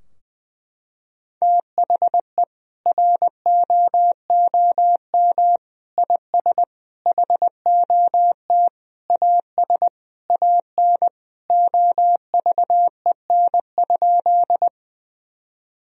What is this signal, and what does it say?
Morse code, heard as a keyed tone, about 21 wpm, 700 Hz. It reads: THE ROOM IS HOT AS AN OVEN?
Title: Sentences from Top 1000 Words - Review Part 4 - 20wpm